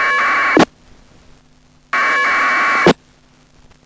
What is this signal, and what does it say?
Packet radio (AX.25, as in APRS).
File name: uploader / iss aprs